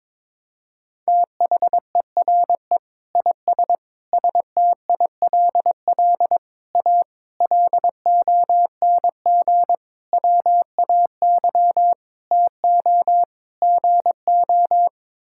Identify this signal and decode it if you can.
Morse code, heard as a keyed tone, about 22 wpm, 700 Hz. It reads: THERE IS STILL A LONG WAY TO GO
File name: practice / 22wpm / Sets of 8 Words - 22wpm